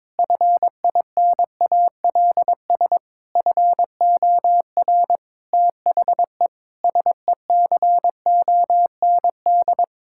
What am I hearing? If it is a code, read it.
Morse code, heard as a keyed tone, about 22 wpm, 700 Hz. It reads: FINALS FOR THE SECOND
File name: Sentences with 7-Word Phrases - 22wpm